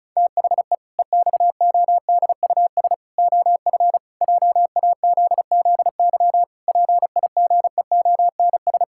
Morse code, heard as a keyed tone, about 34 wpm, 700 Hz. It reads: THE EXODUS OF JAZZY PIGEONS
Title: Pangram Challenge v2 - 35wpm